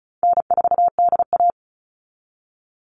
Morse code, heard as a keyed tone, about 34 wpm, 700 Hz. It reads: N4DA
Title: NA SOTA Chaser Callsigns 35wpm